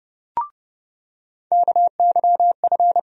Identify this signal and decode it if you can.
Morse code, heard as a keyed tone, about 30 wpm, 700 Hz. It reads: KYF
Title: USA Call Sign Postfixes - 3 Letters 30wpm